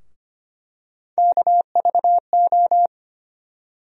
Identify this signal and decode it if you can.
Morse code, heard as a keyed tone, about 25 wpm, 700 Hz. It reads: KVO
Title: USA Call Sign Postfixes - 3 Letters 25wpm